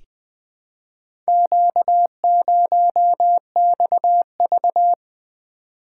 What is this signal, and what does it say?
Morse code, heard as a keyed tone, about 20 wpm, 700 Hz. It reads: Q0XV
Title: Four Letter-Number Combinations - 20wpm